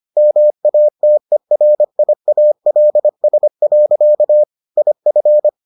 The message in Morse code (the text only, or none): MATERIALS. IF